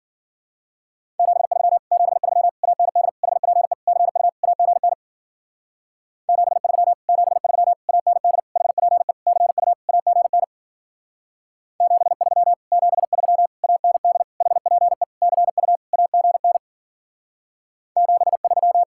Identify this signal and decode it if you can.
Morse code, heard as a keyed tone, about 47 wpm, 700 Hz. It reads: 73 73 AND HPE CU AGN 73 73 AND HPE CU AGN 73 73 AND HPE CU AGN 73
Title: Speed-Racing Common QSO Phrases - 40wpm